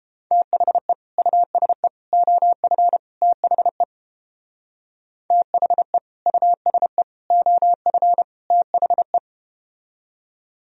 Morse code, heard as a keyed tone, about 31 wpm, 700 Hz. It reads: THE USE OF THE THE USE OF THE